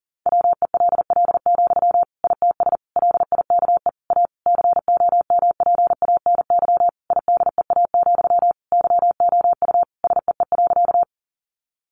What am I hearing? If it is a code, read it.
Morse code, heard as a keyed tone, about 37 wpm, 700 Hz. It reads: WELL, ITS LIKE A COMPANY IDEA, YOU SEE.
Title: Sentences from Top 600 Words 40wpm